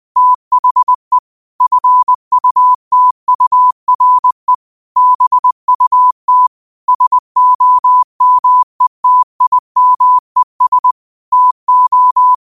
Morse code, heard as a keyed tone, about 20 wpm, 1000 Hz. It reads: THE FUTURE BUT SOMETIMES TO
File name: qotdfiles / 20wpm